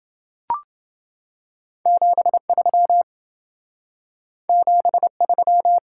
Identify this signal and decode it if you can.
Morse code, heard as a keyed tone, about 28 wpm, 700 Hz. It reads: 73 73